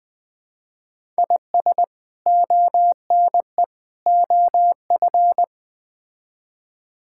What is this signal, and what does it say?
Morse code, heard as a keyed tone, about 20 wpm, 700 Hz. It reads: IS ONE OF
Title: Build A Phrase - 4-Word Phrases - 20wpm